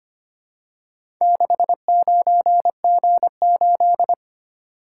Morse code, heard as a keyed tone, about 25 wpm, 700 Hz. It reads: 69G8
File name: practice / 25wpm / Four Letter-Number Combinations - 25wpm